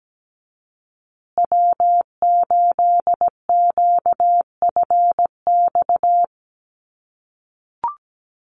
Morse code, heard as a keyed tone, about 17 wpm, 700 Hz. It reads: W8QFX
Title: USA Call Signs Repeated 2x 17wpm